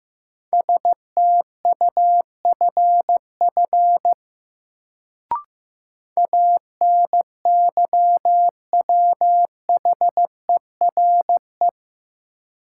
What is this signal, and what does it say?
Morse code, heard as a keyed tone, about 15 wpm, 700 Hz. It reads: STUFF ANYWHERE
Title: Sentences from Top 1400 Words - New Word Drill - 15wpm